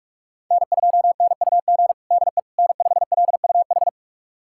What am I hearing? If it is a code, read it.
Morse code, heard as a keyed tone, about 41 wpm, 700 Hz. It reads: N2NUG DE N5LUH
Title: USA Callsign DE Callsign - 5 Characters - 45wpm